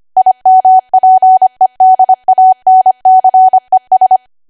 Morse code, heard as a keyed tone, about 25 wpm, 750 Hz. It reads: IMPEDANCES
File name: Archive / 25 WPM / 25WPM